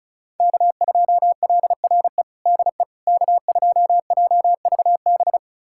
Morse code, heard as a keyed tone, about 34 wpm, 700 Hz. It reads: K2LRE DE K2JVB
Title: USA Callsign DE Callsign - 5 Characters - 35wpm